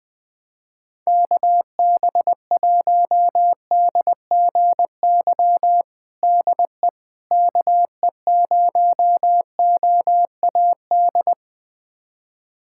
Morse code, heard as a keyed tone, about 21 wpm, 700 Hz. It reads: KB1DGY DE KE0OAD
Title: USA Callsign DE Callsign - 6 Characters - 20wpm